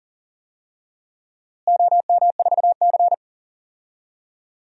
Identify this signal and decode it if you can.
Morse code, heard as a keyed tone, about 38 wpm, 700 Hz. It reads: OM4C
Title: DXCC-100 Call Signs - 4 Characters - Repeated 3x - 40wpm